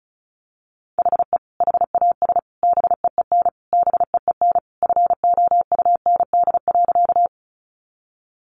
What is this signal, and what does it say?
Morse code, heard as a keyed tone, about 34 wpm, 700 Hz. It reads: HE HAS BEEN BEEN FOUND.
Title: Sentences from Top 300 Words 35wpm